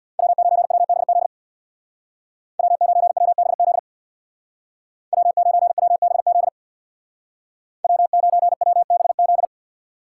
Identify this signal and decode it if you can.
Morse code, heard as a keyed tone, about 60 wpm, 700 Hz. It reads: W9WBZ W9WBZ W9WBZ W9WBZ
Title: Speed Racing - USA 1x3 Call Signs 50wpm